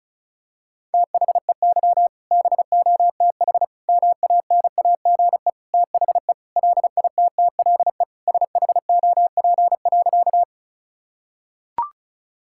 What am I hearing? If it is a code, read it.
Morse code, heard as a keyed tone, about 36 wpm, 700 Hz. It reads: THEY BOTH MANAGE THE LITTLE SHOP.
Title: Sentences from Top 1600 Words 35wpm